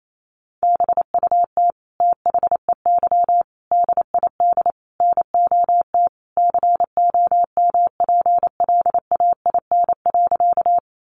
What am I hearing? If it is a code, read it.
Morse code, heard as a keyed tone, about 28 wpm, 700 Hz. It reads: BUT THEY DID NOT COMPLAIN.
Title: Sentences from Top 1300 Words 28wpm